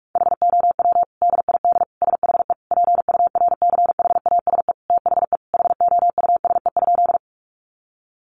Morse code, heard as a keyed tone, about 44 wpm, 700 Hz. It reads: HOW DID SHE PURCHASE THE HOUSE?
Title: Sentences from Top 800 Words 45wpm